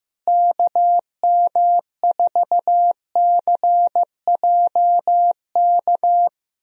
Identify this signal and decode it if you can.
Morse code, heard as a keyed tone, about 15 wpm, 700 Hz. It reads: KM4CJK
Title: USA Call Signs - 6 Characters - Repeated 3x - 15wpm